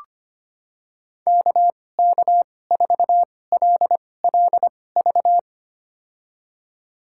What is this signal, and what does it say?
Morse code, heard as a keyed tone, about 25 wpm, spaced as for 18 wpm, 700 Hz. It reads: KK4LLV
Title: USA Call Signs - 6 Characters - 2x Character Spacing - 25wpm